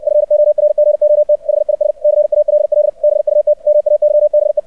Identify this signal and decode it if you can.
Morse code, heard as a keyed tone, about 64 wpm, 600 Hz. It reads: FORGOT HIS CALL BUT MAYBE
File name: archived / CRSnet